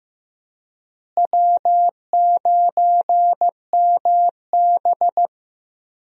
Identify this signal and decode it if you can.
Morse code, heard as a keyed tone, about 15 wpm, 700 Hz. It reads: W9MB
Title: USA Call Signs - 4 Characters - Repeated 3x - 15wpm